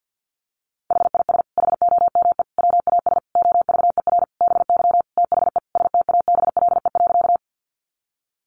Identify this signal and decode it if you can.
Morse code, heard as a keyed tone, about 50 wpm, 700 Hz. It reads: HIS HOME WAS OVER BY THE STABLE.
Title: Sentences from Top 1500 Words 50wpm